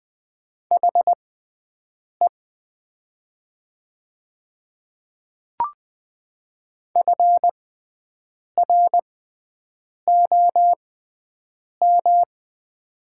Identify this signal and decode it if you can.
Morse code, heard as a keyed tone, about 20 wpm, spaced as for 7 wpm, 700 Hz. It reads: HE FROM
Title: 6x Character Spacing - Top 100 Words - 20wpm